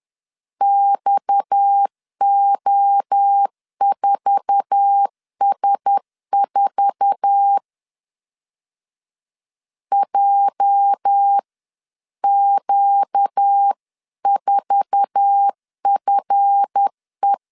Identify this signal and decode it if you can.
Morse code, heard as a keyed tone, about 10 wpm, 800 Hz. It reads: XO4S4 JQ4FE
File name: LEZIONI / 8 YJ45 / 8 YJ4 10wpm 2633